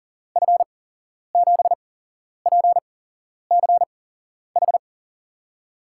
Morse code, heard as a keyed tone, about 40 wpm, spaced as for 11 wpm, 700 Hz. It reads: F7PCH
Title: USA Call Signs - 6 Characters - 8x Character Spacing - 40wpm